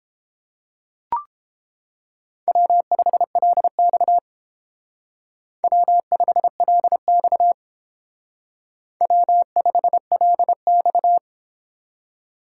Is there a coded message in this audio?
Yes: Morse code from 2.5 to 11.2 s, heard as a keyed tone, about 29 wpm, 700 Hz. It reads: W5LX W5LX W5LX